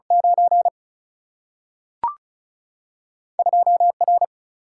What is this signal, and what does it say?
Morse code, heard as a keyed tone, about 36 wpm, 700 Hz. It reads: 9 2 R
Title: Single-Letter Single-Number 35wpm